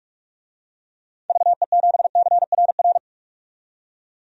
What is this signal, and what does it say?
Morse code, heard as a keyed tone, about 46 wpm, 700 Hz. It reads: VE7CRR
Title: DXCC-100 Call Signs - 6 Characters - 45wpm